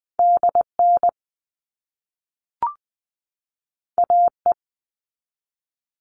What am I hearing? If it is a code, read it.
Morse code, heard as a keyed tone, about 20 wpm, 700 Hz. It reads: DN AE
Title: All Cut Numbers - 2-Digit 20wpm